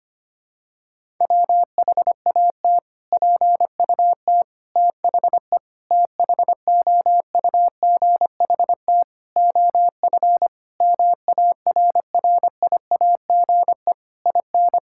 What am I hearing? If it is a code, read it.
Morse code, heard as a keyed tone, about 25 wpm, 700 Hz. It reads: WHAT PUT THE THOUGHT OF MARRIAGE IN
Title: Sentences from Top 1000 Words - Review Part 5 - 25wpm